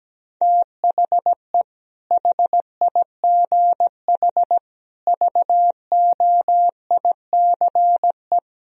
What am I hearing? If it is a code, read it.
Morse code, heard as a keyed tone, about 17 wpm, 700 Hz. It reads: THE HIGH VOICE
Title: Sentences from Top 1000 Words - Review Part 1 - 17wpm